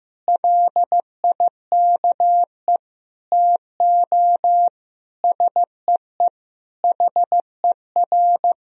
Morse code, heard as a keyed tone, about 15 wpm, 700 Hz. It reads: LIKE TO SEE HER
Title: Sentences from Top 200 Words and Abbreviations 15wpm